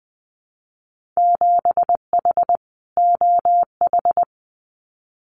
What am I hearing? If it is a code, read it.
Morse code, heard as a keyed tone, about 20 wpm, 700 Hz. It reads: 7H OH
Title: Winter Field Day Exchanges Encore 20wpm